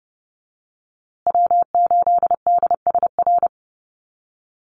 Morse code, heard as a keyed tone, about 30 wpm, 700 Hz. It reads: W8DSR
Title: USA Call Signs Repeated 3x 30wpm